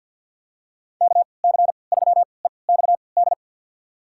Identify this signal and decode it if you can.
Morse code, heard as a keyed tone, about 52 wpm, spaced as for 29 wpm, 700 Hz. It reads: KC3EXD
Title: USA Call Signs - 6 Characters - 3x Character Spacing - 50wpm